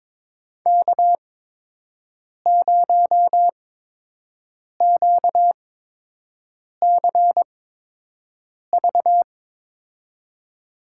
Morse code, heard as a keyed tone, about 22 wpm, spaced as for 6 wpm, 700 Hz. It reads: K0QCV
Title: USA Call Signs - 5 Characters - 8x Character Spacing - 22wpm